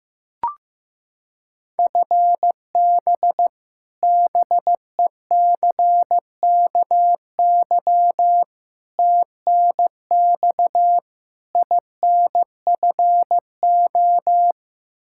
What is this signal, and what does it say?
Morse code, heard as a keyed tone, about 15 wpm, 700 Hz. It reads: FB BECKY TNX INFO
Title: Practice QSO - Exchange 3 - More Info - 15wpm